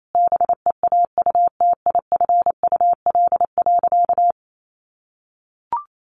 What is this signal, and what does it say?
Morse code, heard as a keyed tone, about 28 wpm, 700 Hz. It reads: BEAUTIFUL.
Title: Sentences from Top 1200 Words 28wpm